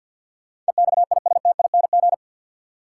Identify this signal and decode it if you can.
Morse code, heard as a keyed tone, about 47 wpm, 700 Hz. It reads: EXISTING
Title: Sentences from Top 1500 Words - New Word Drill - 50wpm